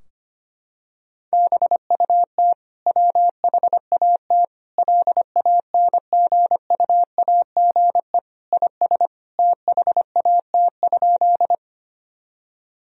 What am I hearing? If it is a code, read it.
Morse code, heard as a keyed tone, about 25 wpm, 700 Hz. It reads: BUT WHAT LANGUAGE IS THAT?
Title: Sentences from Top 1000 Words - Review Part 3 - 25wpm